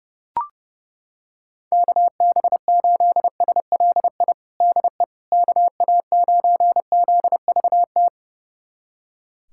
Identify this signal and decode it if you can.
Morse code, heard as a keyed tone, about 29 wpm, 700 Hz. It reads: KB8SLI DE KA9ZVT
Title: USA Callsign DE Callsign - 6 Characters - 30wpm